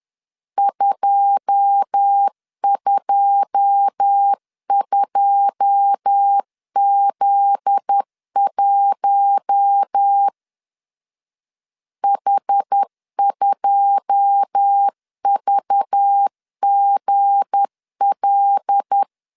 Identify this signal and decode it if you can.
Morse code, heard as a keyed tone, about 10 wpm, 800 Hz. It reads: 222Z1 H2VGL
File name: LEZIONI / 9 2 10wpm 2736